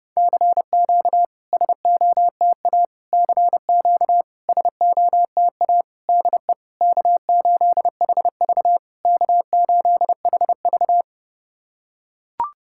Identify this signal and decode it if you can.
Morse code, heard as a keyed tone, about 30 wpm, 700 Hz. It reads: CQ SOTA CQ SOTA DE K8HV K8HV
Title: SOTA QSO Practice 30wpm